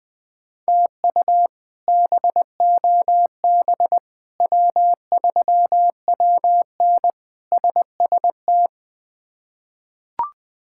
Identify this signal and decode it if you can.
Morse code, heard as a keyed tone, about 20 wpm, 700 Hz. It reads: TU BOB W3WN SST
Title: K1USN SST QSO Practice - 20wpm